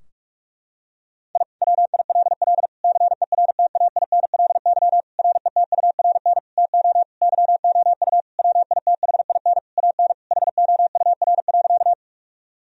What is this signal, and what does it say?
Morse code, heard as a keyed tone, about 46 wpm, 700 Hz. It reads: I WILL CERTAINLY RETURN TO YOU WITHIN AN HOUR.
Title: Sentences from Top 1000 Words - Review Part 4 - 45wpm